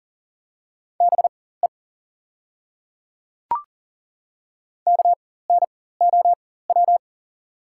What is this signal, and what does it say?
Morse code, heard as a keyed tone, about 40 wpm, spaced as for 19 wpm, 690 Hz. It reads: BE KNOW
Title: 4x Character Spacing - Top 100 Words - 40wpm